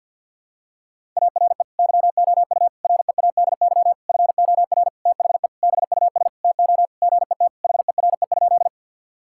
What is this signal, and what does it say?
Morse code, heard as a keyed tone, about 52 wpm, 700 Hz. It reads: ARE YOU READY FOR THE BUS TO GET HERE?
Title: Sentences from Top 1000 Words - Review Part 6 - 50wpm